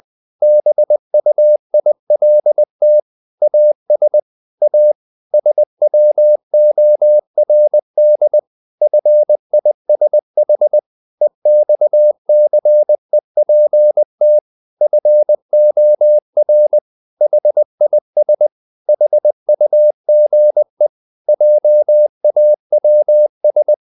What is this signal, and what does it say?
Morse code, heard as a keyed tone, about 20 wpm, 600 Hz. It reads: BUILT AS A SWORDFISH EXCEPT FOR HIS HUGE JAWS
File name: the old man and the sea part 2 20wpm